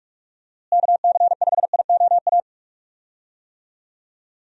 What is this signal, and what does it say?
Morse code, heard as a keyed tone, about 41 wpm, 700 Hz. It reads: KC5IOA
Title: USA Call Signs - 6 Characters - Repeated 3x - 45wpm